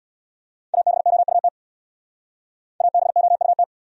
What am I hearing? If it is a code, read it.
Morse code, heard as a keyed tone, about 63 wpm, 700 Hz. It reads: ABOUT ABOUT